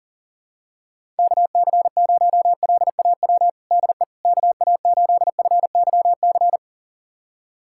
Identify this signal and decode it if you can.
Morse code, heard as a keyed tone, about 40 wpm, 700 Hz. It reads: KC0LAW DE KA8FYC
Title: USA Callsign DE Callsign - 6 Characters - 40wpm